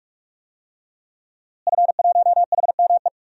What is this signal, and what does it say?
Morse code, heard as a keyed tone, about 46 wpm, 700 Hz. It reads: F1HME
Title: DXCC-100 Call Signs - 5 Characters - Repeated 3x - 45wpm